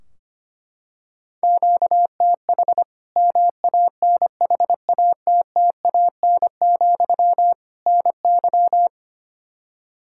Morse code, heard as a keyed tone, about 25 wpm, 700 Hz. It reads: QTH MANHATTAN, NY
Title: Practice QSO - Exchange 1 - Beginning - 25wpm